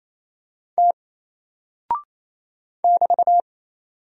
Morse code, heard as a keyed tone, about 28 wpm, 700 Hz. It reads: T <BT>
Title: Single Character (Letter:Number:Punctuation:Prosign) 28wpm